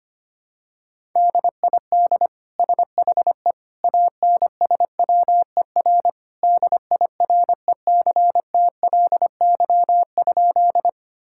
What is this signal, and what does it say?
Morse code, heard as a keyed tone, about 25 wpm, 700 Hz. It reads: DID SHE ANSWER DIRECTLY?
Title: Sentences from Top 1000 Words - Review Part 5 - 25wpm